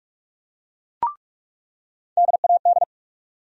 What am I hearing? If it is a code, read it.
Morse code, heard as a keyed tone, about 47 wpm, 700 Hz. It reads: DAD
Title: USA Call Sign Postfixes - 3 Letters 45wpm